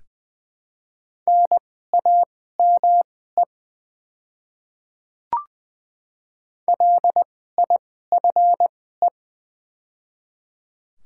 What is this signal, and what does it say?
Morse code, heard as a keyed tone, about 20 wpm, spaced as for 14 wpm, 700 Hz. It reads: NAME LIFE
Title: 2x Character Spacing - Top 300 Words - 20wpm